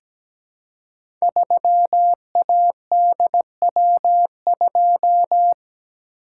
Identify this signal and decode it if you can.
Morse code, heard as a keyed tone, about 17 wpm, 700 Hz. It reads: 3ADW2
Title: Code Groups - 5 Letter:Numbers - 17wpm